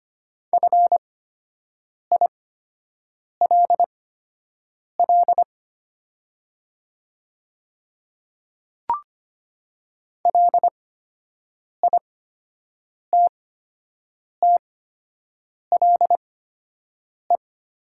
Morse code, heard as a keyed tone, about 25 wpm, spaced as for 7 wpm, 700 Hz. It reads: FILL LITTLE
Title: 8x Character Spacing - Top 500 Words - 25wpm